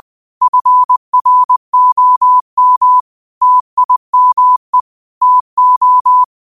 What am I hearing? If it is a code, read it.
Morse code, heard as a keyed tone, about 20 wpm, 1000 Hz. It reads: FROM TIME TO